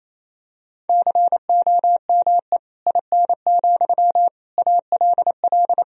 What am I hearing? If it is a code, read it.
Morse code, heard as a keyed tone, about 27 wpm, 700 Hz. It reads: COME IN, ALL
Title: Sentences from Top 1000 Words - Review Part 1 - 28wpm